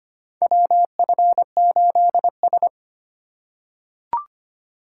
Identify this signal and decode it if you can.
Morse code, heard as a keyed tone, about 25 wpm, 700 Hz. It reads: WF8S
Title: USA Call Signs - 4 Characters - 2x Character Spacing - 25wpm